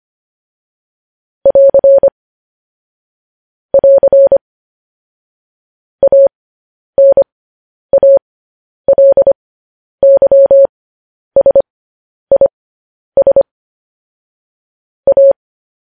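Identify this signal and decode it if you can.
Morse code, heard as a keyed tone, about 25 wpm, spaced as for 10 wpm, 550 Hz. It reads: <AR> <AR> ANALYSIS A
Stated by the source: Morse news headlines at 10.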